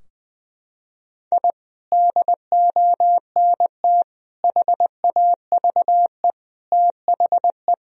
Morse code, heard as a keyed tone, about 20 wpm, 700 Hz. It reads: I DONT HAVE THE
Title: Sentences from Top 1000 Words - Review Part 3 - 20wpm